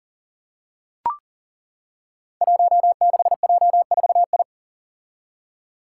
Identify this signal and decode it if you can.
Morse code, heard as a keyed tone, about 38 wpm, 700 Hz. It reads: 16J4I
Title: Five Challenging Letter-Number Combinations 40wpm